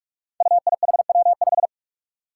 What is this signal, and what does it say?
Morse code, heard as a keyed tone, about 46 wpm, 700 Hz. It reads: UIHW5
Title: Five Challenging Letter-Number Combinations 45wpm